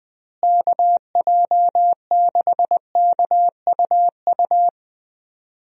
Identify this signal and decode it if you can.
Morse code, heard as a keyed tone, about 20 wpm, 700 Hz. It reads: KJ6KUU
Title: USA Call Signs - 6 Characters - Repeated 3x - 20wpm